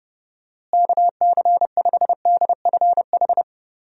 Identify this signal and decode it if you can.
Morse code, heard as a keyed tone, about 30 wpm, 700 Hz. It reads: KC5DFH
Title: USA Call Signs - 6 Characters - Repeated 3x - 30wpm